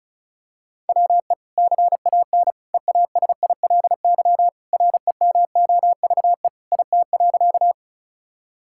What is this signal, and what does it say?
Morse code, heard as a keyed tone, about 36 wpm, 700 Hz. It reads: WE CAN EASILY REMOVE IT.
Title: Sentences from Top 1000 Words - Review Part 5 - 35wpm